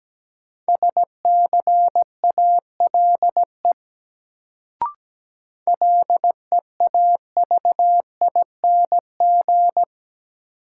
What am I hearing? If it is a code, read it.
Morse code, heard as a keyed tone, about 17 wpm, 700 Hz. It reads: SCALE LEAVING
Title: Sentences from Top 1000 Words - Review Part 4 - New Word Drill - 17wpm